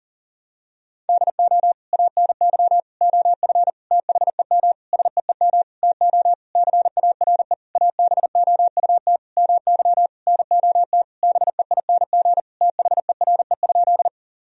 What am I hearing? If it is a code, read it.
Morse code, heard as a keyed tone, about 37 wpm, 700 Hz. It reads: DO ANY OF THEM SEEM TO CARE ABOUT MY NOT BEING THERE?
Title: Sentences from Top 1000 Words - Review Part 3 - 40wpm